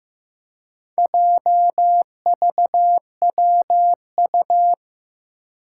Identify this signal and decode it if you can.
Morse code, heard as a keyed tone, about 15 wpm, 700 Hz. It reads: JVWU
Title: Four Challenging Letter-Number Combinations 15wpm